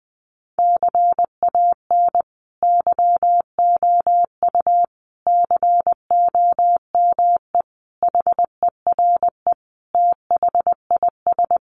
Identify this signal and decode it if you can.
Morse code, heard as a keyed tone, about 20 wpm, 700 Hz. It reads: CAN YOU COME HERE THIS
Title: Sentences from Top 1500 Words 20wpm